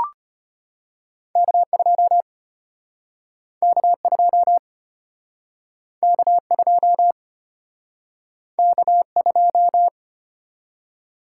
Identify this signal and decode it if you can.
Morse code, heard as a keyed tone, about 35 wpm, 700 Hz. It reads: K2 K2 K2 K2